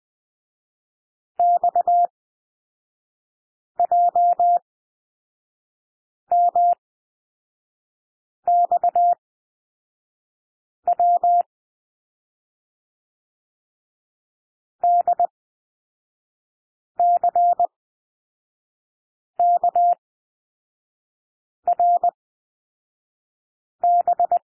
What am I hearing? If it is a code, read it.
Morse code, heard as a keyed tone, about 20 wpm, spaced as for 5 wpm, 700 Hz. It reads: XJMXW DCKRB
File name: SOUND LETTERS 20 wpm